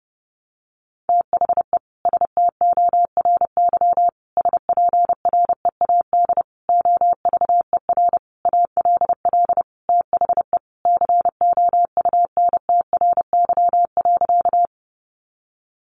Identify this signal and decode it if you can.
Morse code, heard as a keyed tone, about 30 wpm, 700 Hz. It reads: THE STORY SPREAD OVER ALL THE COUNTRY.
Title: Sentences from Top 1200 Words 30wpm